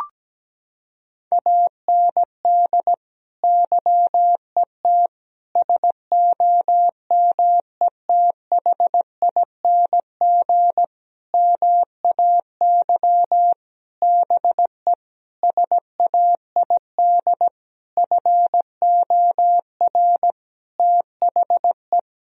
Morse code, heard as a keyed tone, about 17 wpm, 700 Hz. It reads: AND YET SOMETHING MAY BE SAID FOR THE
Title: Sentences from Top 1000 Words - Review Part 3 - 17wpm